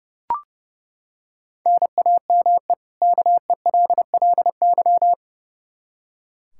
Morse code, heard as a keyed tone, about 31 wpm, 700 Hz. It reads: NAME KELLY
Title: Practice QSO - Exchange 1 - Beginning - 30wpm